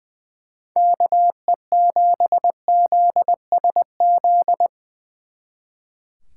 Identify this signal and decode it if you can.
Morse code, heard as a keyed tone, about 23 wpm, 700 Hz. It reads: KE7ZSZ
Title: Speed Racing - USA 2x3 Call Signs 20wpm